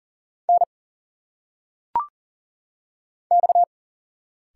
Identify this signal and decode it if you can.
Morse code, heard as a keyed tone, about 40 wpm, 700 Hz. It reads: N X
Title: Single Letters - Phonetic Pronunciation 40wpm